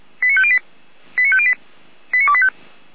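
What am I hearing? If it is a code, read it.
Five-tone selective call (ZVEI), heard as quick runs of five tones, tones of about 70 ms. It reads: 89508 89508 89285
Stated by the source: ZVEI five-tone selective call received by SDR.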